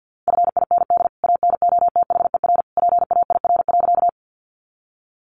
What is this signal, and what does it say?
Morse code, heard as a keyed tone, about 50 wpm, 700 Hz. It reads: FIND ANOTHER PAIR.
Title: Sentences from Top 1200 Words 50wpm